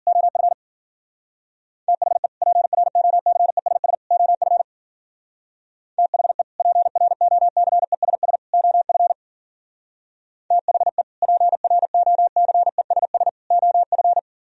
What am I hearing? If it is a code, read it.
Morse code, heard as a keyed tone, about 50 wpm, 700 Hz. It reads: OF THE PROCESS OF THE PROCESS OF THE PROCESS OF